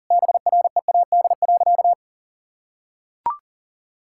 Morse code, heard as a keyed tone, about 40 wpm, 700 Hz. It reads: BREAD.
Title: Sentences from Top 1600 Words 40wpm